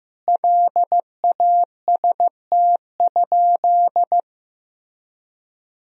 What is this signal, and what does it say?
Morse code, heard as a keyed tone, about 15 wpm, 700 Hz. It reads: LAST?
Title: Sentences from Top 1000 Words - Review Part 4 - 15wpm